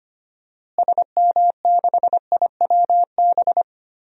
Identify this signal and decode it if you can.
Morse code, heard as a keyed tone, about 25 wpm, 700 Hz. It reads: SM6IWB
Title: DXCC-100 Call Signs - 6 Characters - Repeated 2x - 25wpm